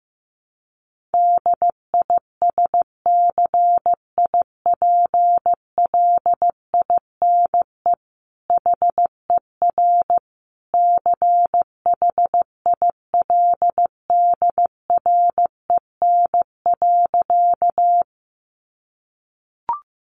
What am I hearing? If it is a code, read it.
Morse code, heard as a keyed tone, about 15 wpm, 700 Hz. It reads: DISCIPLINE HER CHILDREN.
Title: Sentences from Top 900 Words 15wpm